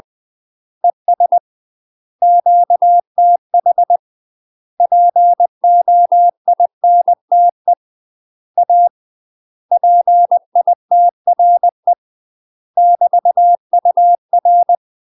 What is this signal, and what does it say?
Morse code, heard as a keyed tone, about 20 wpm, 700 Hz. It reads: ES QTH POINTE A PITRE <BT> UR
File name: UK MASTER QSOs TRAINING / 20WPM / UK MASTER QSOs 0034